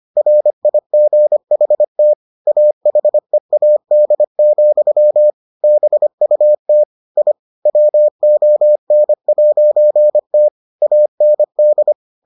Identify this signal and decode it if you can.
Morse code, heard as a keyed tone, about 25 wpm, 600 Hz. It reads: RIGHT AHEAD, BUT I WON'T AND